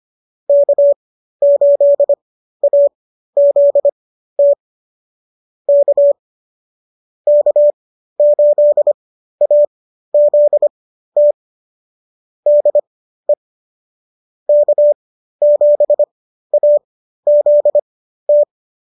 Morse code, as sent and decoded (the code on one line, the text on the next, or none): -.- ---.. .- --.. - / -.- / -.- ---.. .- --.. - / -.. / -.- --... .- --.. -
K8AZT K K8AZT DE K7AZT